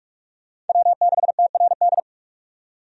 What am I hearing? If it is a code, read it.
Morse code, heard as a keyed tone, about 46 wpm, 700 Hz. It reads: W6TRD
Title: USA Call Signs - 5 Characters - 45wpm